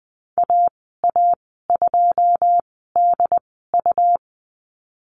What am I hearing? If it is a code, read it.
Morse code, heard as a keyed tone, about 19 wpm, spaced as for 14 wpm, 700 Hz. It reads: AA2DU
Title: 122 Lesson 20g - Number 3 - Call Signs 20wpm